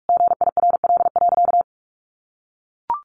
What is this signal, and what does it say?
Morse code, heard as a keyed tone, about 44 wpm, 700 Hz. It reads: GIRL.
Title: Sentences from Top 400 Words 45wpm